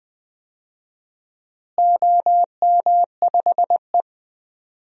Morse code, heard as a keyed tone, about 20 wpm, 700 Hz. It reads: OM5E
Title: DXCC-100 Call Signs - 4 Characters - Repeated 2x - 20wpm